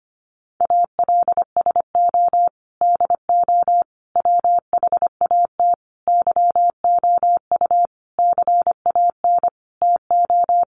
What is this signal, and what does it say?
Morse code, heard as a keyed tone, about 25 wpm, 700 Hz. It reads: ALSO DO WHAT YOU CAN TO
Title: Sentences from Top 1300 Words 25wpm